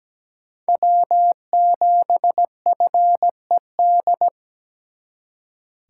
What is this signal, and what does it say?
Morse code, heard as a keyed tone, about 20 wpm, 700 Hz. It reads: W7FED
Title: Speed Racing - USA 1x3 Call Signs 17wpm